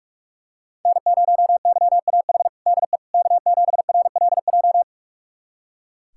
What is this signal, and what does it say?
Morse code, heard as a keyed tone, about 43 wpm, 700 Hz. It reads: N0YAH DE K7RLJ
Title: USA Callsign DE Callsign - 5 Characters - 45wpm